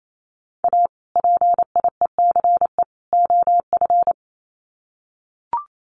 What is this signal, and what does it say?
Morse code, heard as a keyed tone, about 28 wpm, 700 Hz. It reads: A PIECE OF
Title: Sets of 3 Words 28wpm